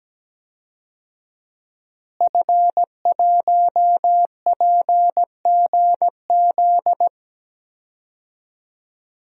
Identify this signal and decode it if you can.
Morse code, heard as a keyed tone, about 17 wpm, 700 Hz. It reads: F1PGZ
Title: DXCC-100 Call Signs - 5 Characters - Repeated 2x - 17wpm